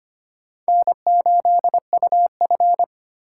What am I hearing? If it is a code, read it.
Morse code, heard as a keyed tone, about 25 wpm, 700 Hz. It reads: N8UF
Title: USA Call Signs - 4 Characters - 2x Character Spacing - 25wpm